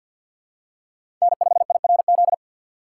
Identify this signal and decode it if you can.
Morse code, heard as a keyed tone, about 53 wpm, 700 Hz. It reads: N5IRZ
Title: USA Call Signs - 5 Characters - 3x Character Spacing - 50wpm